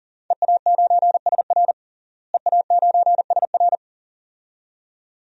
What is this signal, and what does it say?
Morse code, heard as a keyed tone, about 40 wpm, 700 Hz. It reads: EA9SR EA9SR
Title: DXCC-100 Call Signs - 5 Characters - Repeated 2x - 40wpm